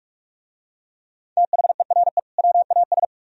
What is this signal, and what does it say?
Morse code, heard as a keyed tone, about 44 wpm, 700 Hz. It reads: THERE WAS